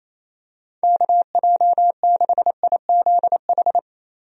Morse code, heard as a keyed tone, about 28 wpm, 700 Hz. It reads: KJ6IZH
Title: USA Call Signs - 6 Characters - Repeated 2x - 28wpm